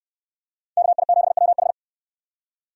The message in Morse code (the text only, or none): KI7WH